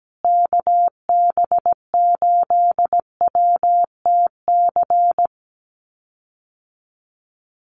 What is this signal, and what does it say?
Morse code, heard as a keyed tone, about 17 wpm, 700 Hz. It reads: KB8WTC
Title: USA Call Signs Repeated 2x 17wpm